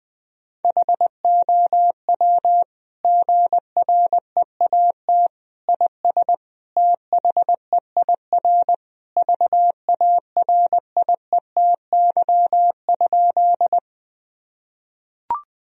Morse code, heard as a keyed tone, about 21 wpm, 700 Hz. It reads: HOW GREAT IS THEIR VARIETY?
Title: Sentences from Top 1000 Words - Review Part 4 - 20wpm